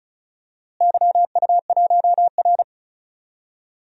Morse code, heard as a keyed tone, about 36 wpm, 700 Hz. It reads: YU1R
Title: DXCC-100 Call Signs - 4 Characters - Repeated 2x - 35wpm